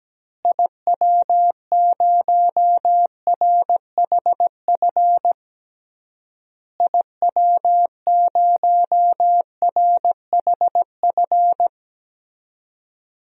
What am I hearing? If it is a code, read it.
Morse code, heard as a keyed tone, about 17 wpm, 700 Hz. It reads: IW0RHF IW0RHF
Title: DXCC-100 Call Signs - 6 Characters - Repeated 3x - 17wpm